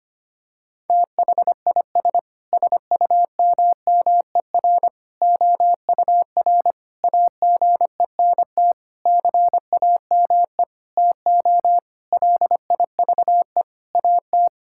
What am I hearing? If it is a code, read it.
Morse code, heard as a keyed tone, about 25 wpm, 700 Hz. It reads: THIS SUMMER OUR AGENT CAME TO LIVE AT
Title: Sentences from Top 1600 Words 25wpm